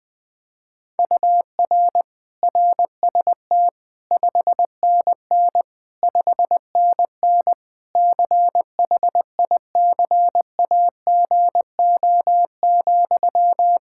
Morse code, heard as a keyed tone, about 20 wpm, 700 Hz. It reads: UR RST 5NN 5NN CHICAGO,
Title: K1USN SST QSO Practice 20wpm